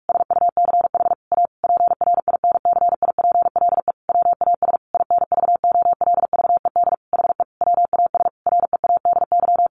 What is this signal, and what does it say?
Morse code, heard as a keyed tone, about 44 wpm, 700 Hz. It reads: SUCH A PRINCIPLE WAS INVOLVED HE WAS READY
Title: Sentences from Top 1400 Words 45wpm